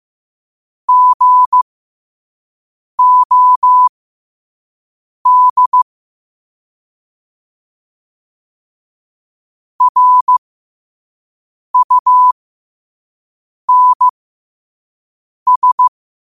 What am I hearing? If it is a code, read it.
Morse code, heard as a keyed tone, about 15 wpm, spaced as for 5 wpm, 1000 Hz. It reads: GOD RUNS
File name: qotdfiles / 5wpm